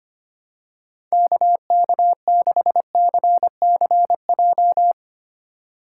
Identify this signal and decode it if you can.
Morse code, heard as a keyed tone, about 25 wpm, 700 Hz. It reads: KK6CCJ
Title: USA Call Signs - 6 Characters - 3x Character Spacing - 25wpm